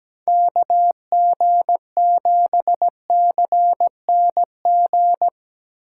Morse code, heard as a keyed tone, about 17 wpm, 700 Hz. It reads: KG7CNG
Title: USA Call Signs - 2x Character Spacing - 17wpm